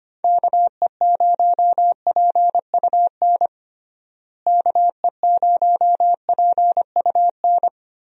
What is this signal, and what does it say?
Morse code, heard as a keyed tone, about 25 wpm, 700 Hz. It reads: KE0PUN KE0PUN
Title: USA Call Signs - 6 Characters - Repeated 2x - 25wpm